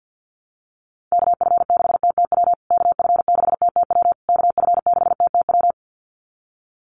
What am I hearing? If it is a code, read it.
Morse code, heard as a keyed tone, about 50 wpm, 700 Hz. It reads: KF6TTW KF6TTW KF6TTW
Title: USA Call Signs Repeated 3x 50wpm